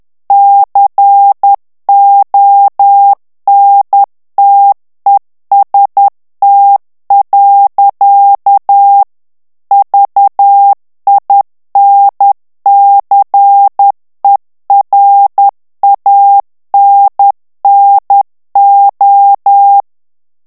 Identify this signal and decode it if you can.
Morse code, heard as a keyed tone, about 10 wpm, 800 Hz. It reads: CONTEST. VINCERANNO
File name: ITALIEN texte clair 2 vitesse 10 mots minute